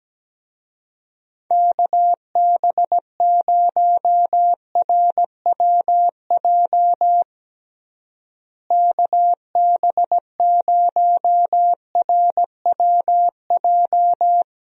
Morse code, heard as a keyed tone, about 17 wpm, 700 Hz. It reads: KB0RWJ KB0RWJ
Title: USA Call Signs - 6 Characters - Repeated 2x - 17wpm